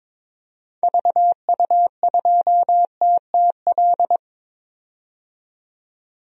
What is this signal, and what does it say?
Morse code, heard as a keyed tone, about 22 wpm, 700 Hz. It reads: VU2TTL
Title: DXCC-100 Call Signs - 6 Characters - 22wpm